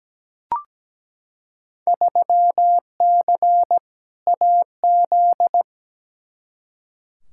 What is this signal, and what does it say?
Morse code, heard as a keyed tone, about 17 wpm, 700 Hz. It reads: 3C AZ
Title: Field Day Exchanges Encore - 17wpm